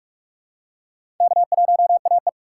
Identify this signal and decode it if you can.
Morse code, heard as a keyed tone, about 43 wpm, 700 Hz. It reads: K1AE
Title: K1USN SST QSO Practice - 45wpm